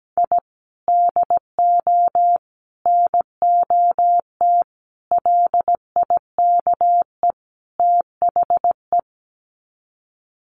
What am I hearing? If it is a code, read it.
Morse code, heard as a keyed tone, about 17 wpm, 700 Hz. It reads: I DO NOT LIKE THE
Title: Sets of 5 Words Encore 17wpm